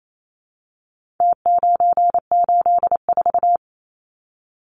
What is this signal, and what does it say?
Morse code, heard as a keyed tone, about 28 wpm, 700 Hz. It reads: T984
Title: Numbers - Contest Style - 4-Digit 28wpm